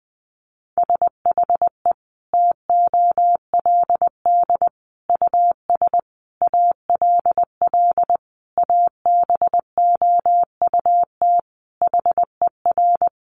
Morse code, heard as a keyed tone, about 20 wpm, 700 Hz. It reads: SHE TOLD US ALL ABOUT HER
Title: Sentences from Top 1200 Words 20wpm